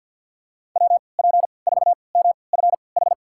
Morse code, heard as a keyed tone, about 48 wpm, spaced as for 28 wpm, 700 Hz. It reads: WP4MFH
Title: USA Call Signs - 3x Character Spacing - 50wpm